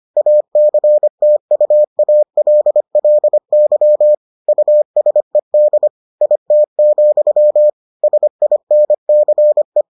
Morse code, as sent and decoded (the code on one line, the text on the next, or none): .- -.-. - ..- .- .-.. .-.. -.-- / ..- ... -.. / .. - --..-- / ... .. -. -.-. .
ACTUALLY USED IT, SINCE